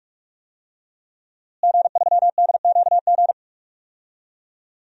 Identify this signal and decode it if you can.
Morse code, heard as a keyed tone, about 41 wpm, 700 Hz. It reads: G3DQG
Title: DXCC-100 Call Signs - 5 Characters - Repeated 3x - 45wpm